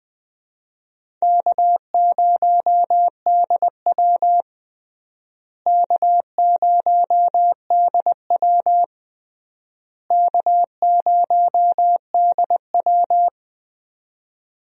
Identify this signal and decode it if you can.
Morse code, heard as a keyed tone, about 20 wpm, 700 Hz. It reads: K0DW K0DW K0DW
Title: USA Call Signs - 4 Characters - Repeated 3x - 20wpm